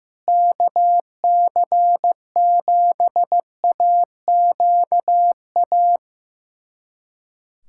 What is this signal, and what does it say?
Morse code, heard as a keyed tone, about 15 wpm, 700 Hz. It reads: KC7AQA
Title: USA Callsign DE Callsign - 6 Characters - 15wpm